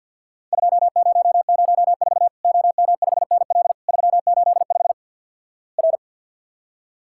Morse code, heard as a keyed tone, about 50 wpm, 700 Hz. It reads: 2004 OM5NL 385 R
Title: Worked All Europe DX Contest - QTC Exchange - 50wpm